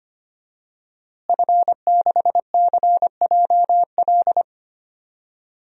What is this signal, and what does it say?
Morse code, heard as a keyed tone, about 25 wpm, 700 Hz. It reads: F6CJL
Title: DXCC-100 Call Signs - 5 Characters - Repeated 3x - 25wpm